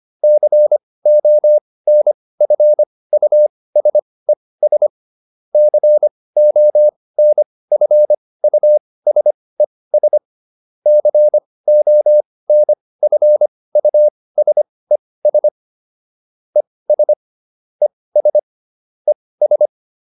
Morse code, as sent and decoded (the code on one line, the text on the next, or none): -.-. --- -. ..-. ..- ... ... / -.-. --- -. ..-. ..- ... ... / -.-. --- -. ..-. ..- ... ... / . ... / . ... / . ...
CONFUSES CONFUSES CONFUSES ES ES ES